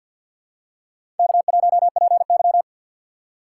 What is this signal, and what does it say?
Morse code, heard as a keyed tone, about 46 wpm, 700 Hz. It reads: K1PY
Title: USA Call Signs - 4 Characters - 2x Character Spacing - 50wpm